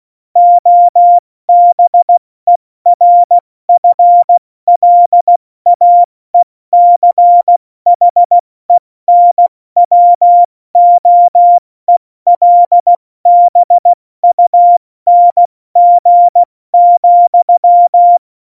Morse code, heard as a keyed tone, about 16 wpm, 700 Hz. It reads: OBERFLAECHENWOELBUNG,
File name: Morsekurs / Witze / Witz 26. Januar 2021, 70 F80